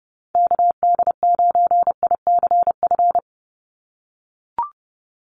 Morse code, heard as a keyed tone, about 28 wpm, 700 Hz. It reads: KD9ICF
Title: USA Call Signs 30wpm